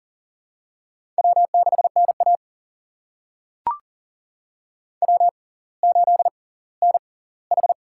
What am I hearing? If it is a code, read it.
Morse code, heard as a keyed tone, about 38 wpm, spaced as for 20 wpm, 700 Hz. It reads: W6NA W8NH
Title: USA Call Signs - 4 Characters - 6x Character Spacing - 40wpm